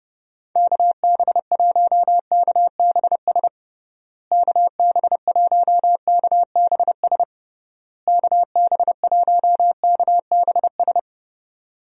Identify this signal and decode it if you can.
Morse code, heard as a keyed tone, about 30 wpm, 700 Hz. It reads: KB1KBS KB1KBS KB1KBS
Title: USA Call Signs - 6 Characters - Repeated 3x - 30wpm